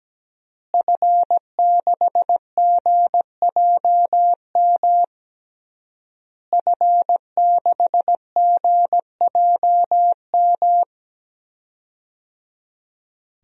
Morse code, heard as a keyed tone, about 17 wpm, 700 Hz. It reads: F6GJM F6GJM
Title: DXCC-100 Call Signs - 5 Characters - Repeated 3x - 17wpm